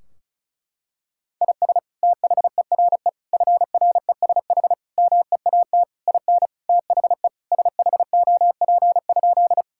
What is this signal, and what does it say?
Morse code, heard as a keyed tone, about 36 wpm, 700 Hz. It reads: IS THERE FRESH MEAT IN THE SHOP?
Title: Sentences from Top 1600 Words 35wpm